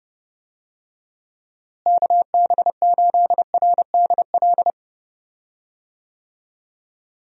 Morse code, heard as a keyed tone, about 32 wpm, 700 Hz. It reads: KB8RDL
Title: USA Call Signs - 6 Characters - Repeated 2x - 30wpm